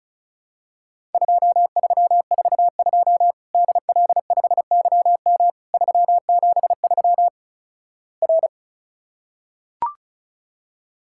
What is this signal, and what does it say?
Morse code, heard as a keyed tone, about 36 wpm, 700 Hz. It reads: 2342 DL5YM 373 R
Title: Worked All Europe DX Contest - QTC Exchange - 35wpm